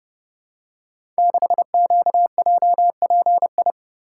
Morse code, heard as a keyed tone, about 30 wpm, 700 Hz. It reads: 6QJPI
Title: Code Groups - 5 Letter:Numbers - 30wpm